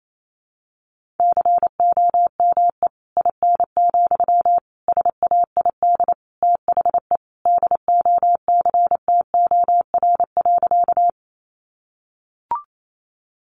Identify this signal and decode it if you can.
Morse code, heard as a keyed tone, about 28 wpm, 700 Hz. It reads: COME IN, SAID THE DOCTOR.
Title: Sentences from Top 1500 Words 28wpm